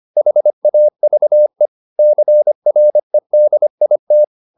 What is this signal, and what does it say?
Morse code, heard as a keyed tone, about 25 wpm, 600 Hz. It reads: HAVE CREDIT
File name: the old man and the sea part 1 25wpm